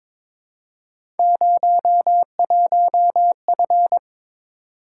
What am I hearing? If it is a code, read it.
Morse code, heard as a keyed tone, about 23 wpm, 700 Hz. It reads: 01F
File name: practice / 22wpm / Three Letter-Number Combinations - 22wpm